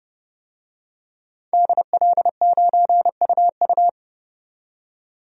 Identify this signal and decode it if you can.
Morse code, heard as a keyed tone, about 30 wpm, 700 Hz. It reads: DL9UU
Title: DXCC-100 Call Signs - 5 Characters - 30wpm